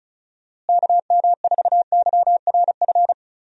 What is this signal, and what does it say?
Morse code, heard as a keyed tone, about 36 wpm, 700 Hz. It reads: KM4YRF
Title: USA Call Signs - 6 Characters - Repeated 3x - 35wpm